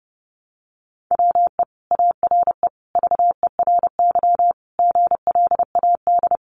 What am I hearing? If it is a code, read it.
Morse code, heard as a keyed tone, about 30 wpm, 700 Hz. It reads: WE ARE VERY GLAD
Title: Sentences from Top 800 Words 30wpm